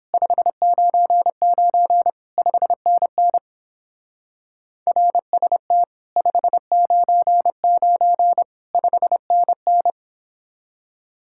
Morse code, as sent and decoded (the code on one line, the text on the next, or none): ..... ----. ----. / ..... -. -. / .-. ... - / ..... ----. ----. / ..... -. -.
599 5NN RST 599 5NN